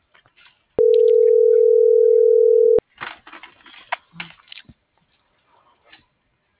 Ringback tone (North America), 440 + 480 Hz.